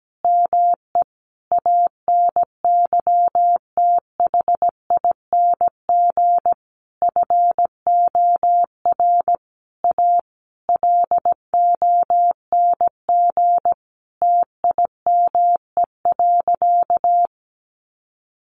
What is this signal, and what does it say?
Morse code, heard as a keyed tone, about 17 wpm, 700 Hz. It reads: ME ANYTHING FOR A LONG TIME.
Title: Sentences from Top 1000 Words 17wpm